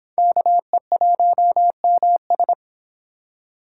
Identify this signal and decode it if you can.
Morse code, heard as a keyed tone, about 26 wpm, 700 Hz. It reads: KE1MS